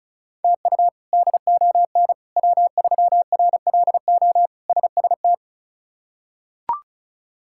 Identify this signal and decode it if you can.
Morse code, heard as a keyed tone, about 36 wpm, 700 Hz. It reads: TU DON W3RLO SST
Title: K1USN SST QSO Practice - 35wpm